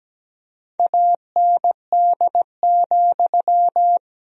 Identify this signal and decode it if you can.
Morse code, heard as a keyed tone, about 17 wpm, 700 Hz. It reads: AND,
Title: Sentences from Top 1000 Words - Review Part 2 - 17wpm